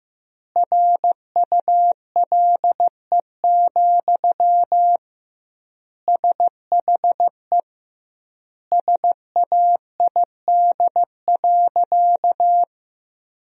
Morse code, heard as a keyed tone, about 15 wpm, 700 Hz. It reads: RULE, SHE SAID.
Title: 2x Word-Spacing - Sentences from Top 400 Words - 15wpm